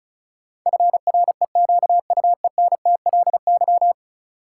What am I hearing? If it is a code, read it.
Morse code, heard as a keyed tone, about 34 wpm, 700 Hz. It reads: FREQUENTLY
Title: Sentences from Top 1300 Words - New Word Drill Alt - 35wpm